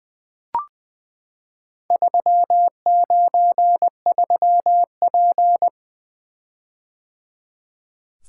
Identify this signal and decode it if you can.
Morse code, heard as a keyed tone, about 20 wpm, 700 Hz. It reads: 393P
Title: Four Letter-Number Combinations - 20wpm